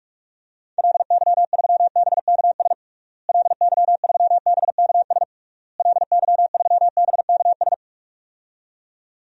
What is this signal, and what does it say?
Morse code, heard as a keyed tone, about 46 wpm, 700 Hz. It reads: LY3BKS LY3BKS LY3BKS
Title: DXCC-100 Call Signs - 6 Characters - Repeated 3x - 45wpm